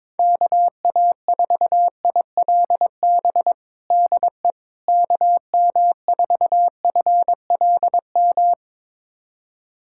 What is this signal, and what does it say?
Morse code, heard as a keyed tone, about 21 wpm, 700 Hz. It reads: KA4ILB DE KM4FLM
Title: USA Callsign DE Callsign 22wpm